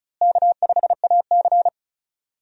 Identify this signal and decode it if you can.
Morse code, heard as a keyed tone, about 36 wpm, 700 Hz. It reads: K5AC
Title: USA Call Signs - 4 Characters - Repeated 2x - 35wpm